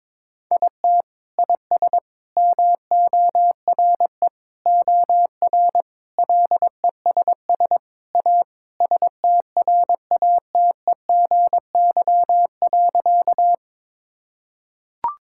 Morse code, heard as a keyed tone, about 22 wpm, 700 Hz. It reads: IT IS MORE OR LESS A STRATEGY.
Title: Sentences from Top 1000 Words - Review Part 5 - 22wpm